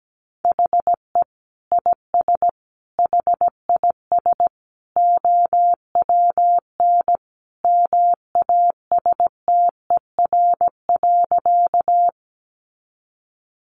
Morse code, heard as a keyed tone, about 17 wpm, 700 Hz. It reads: HE IS HIS OWN MASTER.
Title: Sentences from Top 1300 Words 17wpm